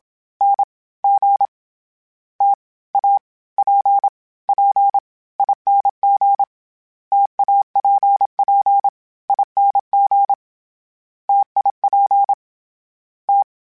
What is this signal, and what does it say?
Morse code, heard as a keyed tone, about 27 wpm, 800 Hz. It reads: N G T A P P ING TAPP ING TIP T
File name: Variable Speed / Variable Speed-mp3s / T AOEI's sp @25